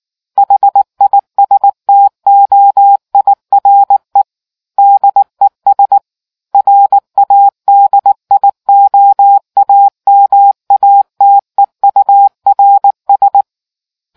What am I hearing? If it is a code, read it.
Morse code, heard as a keyed tone, about 19 wpm, 800 Hz. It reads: HISTOIRE DES RADIOAMATEURS